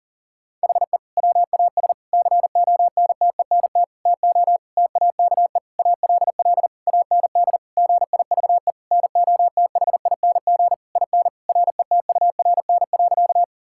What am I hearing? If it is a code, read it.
Morse code, heard as a keyed tone, about 40 wpm, 700 Hz. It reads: HE WAS CONTENT TO TAKE ALL AND GIVE NOTHING IN RETURN.
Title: Sentences from Top 1000 Words - Review Part 6 - 40wpm